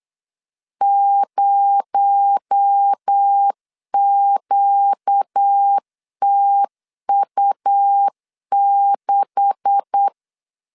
Morse code, heard as a keyed tone, about 8 wpm, 800 Hz. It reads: 0QTU6
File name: LEZIONI / 10 SIGNS / 10C DOT 8wpm 3649